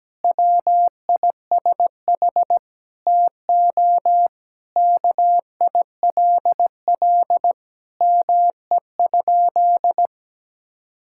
Morse code, heard as a keyed tone, about 17 wpm, 700 Hz. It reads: WISH TO KILL ME?
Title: Sentences from Top 1600 Words 17wpm